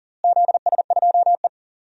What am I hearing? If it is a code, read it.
Morse code, heard as a keyed tone, about 40 wpm, 700 Hz. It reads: ZS2E
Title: DXCC-100 Call Signs - 4 Characters - Repeated 3x - 40wpm